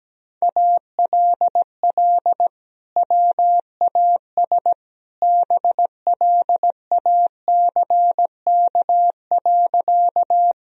Morse code, heard as a keyed tone, about 17 wpm, 700 Hz. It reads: ALL WAS BLACK.
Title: Sentences from Top 1000 Words - Review Part 2 - 17wpm